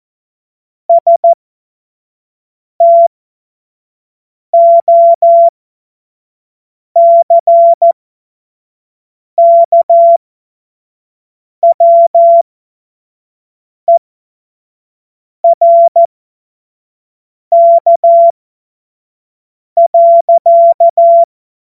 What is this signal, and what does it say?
Morse code, heard as a keyed tone, about 14 wpm, spaced as for 5 wpm, 680 Hz. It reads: STOCKWERK.
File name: Morsekurs / Witze / Witz 02 Januar 2024 25 F70